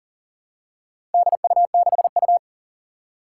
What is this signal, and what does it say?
Morse code, heard as a keyed tone, about 40 wpm, 700 Hz. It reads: DU6U
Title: Four Challenging Letter-Number Combinations 40wpm